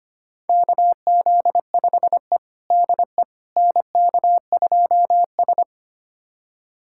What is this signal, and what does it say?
Morse code, heard as a keyed tone, about 25 wpm, 700 Hz. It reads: KZ5E DE NK2S
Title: USA Callsign DE Callsign - 4 Characters - 25wpm